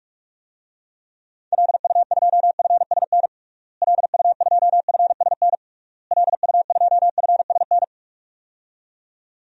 Morse code, heard as a keyed tone, about 41 wpm, 700 Hz. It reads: LU2FSN LU2FSN LU2FSN
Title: DXCC-100 Call Signs - 6 Characters - Repeated 3x - 45wpm